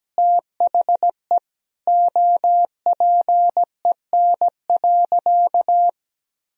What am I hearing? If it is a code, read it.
Morse code, heard as a keyed tone, about 17 wpm, 700 Hz. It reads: THE OPEN.
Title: Sentences from Top 1000 Words - Review Part 1 - 17wpm